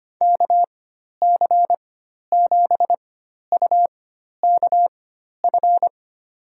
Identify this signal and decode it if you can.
Morse code, heard as a keyed tone, about 25 wpm, spaced as for 12 wpm, 700 Hz. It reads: KC7UKF
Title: USA Call Signs - 4x Character Spacing - 25wpm